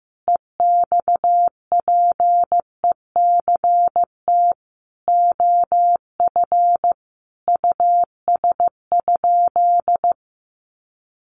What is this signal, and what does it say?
Morse code, heard as a keyed tone, about 15 wpm, 700 Hz. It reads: EXPECT OF US?
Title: Sentences from Top 1400 Words 15wpm